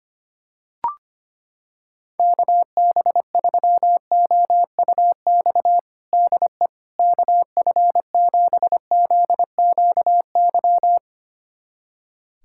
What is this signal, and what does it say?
Morse code, heard as a keyed tone, about 25 wpm, 700 Hz. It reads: KB3OUX DE KF7ZQY
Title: USA Callsign DE Callsign - 6 Characters - 25wpm